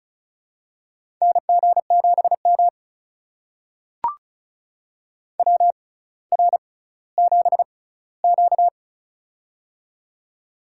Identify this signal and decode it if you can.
Morse code, heard as a keyed tone, about 36 wpm, spaced as for 18 wpm, 700 Hz. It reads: NG7M WR7Q
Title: USA Call Signs - 4 Characters - 6x Character Spacing - 35wpm